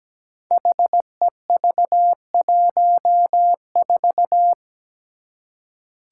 Morse code, heard as a keyed tone, about 17 wpm, 700 Hz. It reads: HEV14
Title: Five Challenging Letter-Number Combinations 17wpm